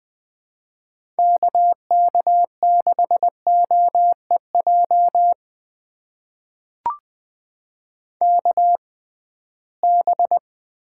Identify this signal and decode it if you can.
Morse code, heard as a keyed tone, about 20 wpm, spaced as for 15 wpm, 700 Hz. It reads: KK6OEJ KB
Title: USA Call Signs - 6 Characters - 6x Character Spacing - 20wpm